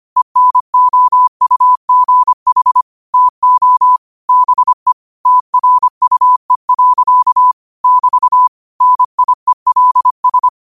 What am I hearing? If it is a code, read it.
Morse code, heard as a keyed tone, about 25 wpm, 1000 Hz. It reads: ENOUGH TO BE TRUE. <BT> NIELS